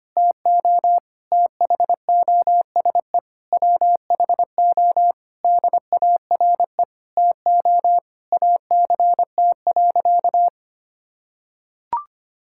Morse code, heard as a keyed tone, about 25 wpm, 700 Hz. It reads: TO THOSE WHO DARE TO ACT.
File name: practice / 25wpm / Difficult Sentences with Hints - 25wpm